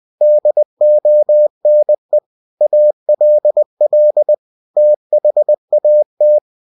Morse code, heard as a keyed tone, about 20 wpm, 600 Hz. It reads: DONE ALL THAT
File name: cw-books / winnie-the-pooh / winnie-20wpm-page-0093